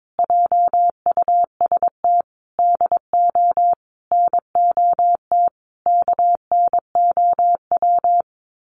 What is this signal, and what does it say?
Morse code, heard as a keyed tone, about 22 wpm, 700 Hz. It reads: JUST DO NOT KNOW
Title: Sets of 5 Words Encore 22wpm